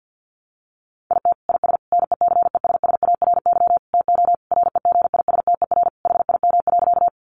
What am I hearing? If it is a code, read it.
Morse code, heard as a keyed tone, about 50 wpm, 700 Hz. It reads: IT IS NECESSARY TO REGISTER HIM.
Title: Sentences from Top 1200 Words 50wpm